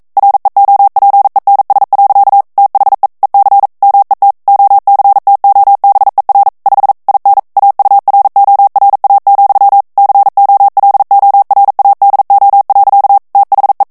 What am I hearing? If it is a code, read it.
Morse code, heard as a keyed tone, about 42 wpm, 800 Hz. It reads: REOPENS. THE EC MET OCTOBER 5 IN AURORA, COLORADO. THE
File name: 40 mots minute